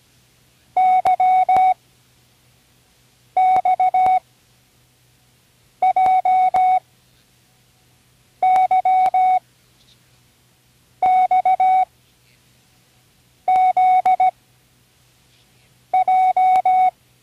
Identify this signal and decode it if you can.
Morse code, heard as a keyed tone, about 17 wpm, 750 Hz. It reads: Y X J Y X Z J
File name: Learning Morse Code (from Tune in the World)/27 random